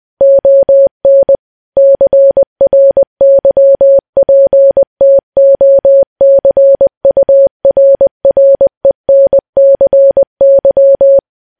Morse code, heard as a keyed tone, about 20 wpm, 550 Hz. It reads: ON CRYPTOCURRENCY
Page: news-headlines-in-morse-code-at-20-wpm